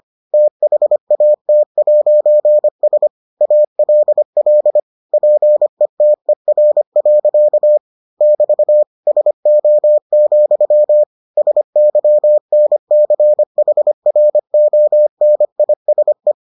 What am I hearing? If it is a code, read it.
Morse code, heard as a keyed tone, about 25 wpm, 600 Hz. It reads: THAT'S ALL PETER. <BT> SO, SYNCHRONISE